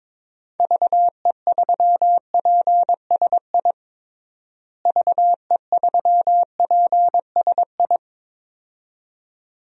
Morse code, heard as a keyed tone, about 22 wpm, 700 Hz. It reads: VE3PSI VE3PSI
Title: DXCC-100 Call Signs - 6 Characters - Repeated 2x - 22wpm